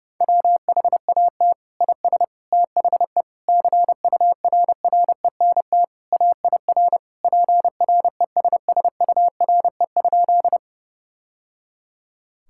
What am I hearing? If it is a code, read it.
Morse code, heard as a keyed tone, about 30 wpm, 700 Hz. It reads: WHAT IS THE CURRENT AIR PRESSURE?
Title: Sentences from Top 1000 Words - Review Part 6 - 30wpm